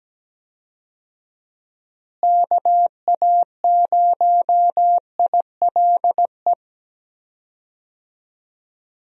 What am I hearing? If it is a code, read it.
Morse code, heard as a keyed tone, about 17 wpm, 700 Hz. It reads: KA0ILE
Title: USA Call Signs - 6 Characters - Repeated 2x - 17wpm